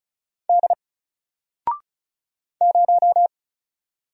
Morse code, heard as a keyed tone, about 34 wpm, 700 Hz. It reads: D 0
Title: Single Letter-Number Alt 35wpm